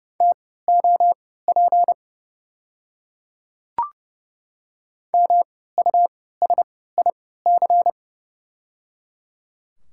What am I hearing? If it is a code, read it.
Morse code, heard as a keyed tone, about 30 wpm, spaced as for 17 wpm, 700 Hz. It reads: TOP MUSIC
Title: 3x Character Spacing - Top 500 Words - 30wpm